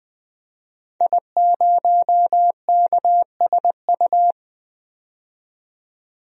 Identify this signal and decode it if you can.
Morse code, heard as a keyed tone, about 20 wpm, 700 Hz. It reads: I0KSU
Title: DXCC-100 Call Signs - 5 Characters - 20wpm